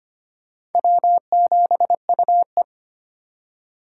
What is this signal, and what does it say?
Morse code, heard as a keyed tone, about 34 wpm, 700 Hz. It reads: W7UE